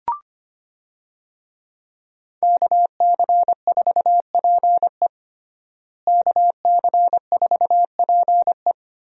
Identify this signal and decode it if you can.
Morse code, heard as a keyed tone, about 25 wpm, 700 Hz. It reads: KC4PE KC4PE
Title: USA Call Signs - 5 Characters - Repeated 2x - 25wpm